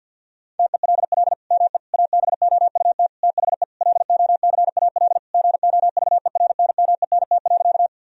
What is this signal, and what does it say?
Morse code, heard as a keyed tone, about 50 wpm, 700 Hz. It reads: TELL ME ABOUT THE LOCAL GOVERNMENT.
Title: Sentences from Top 1000 Words - Review Part 3 - 50wpm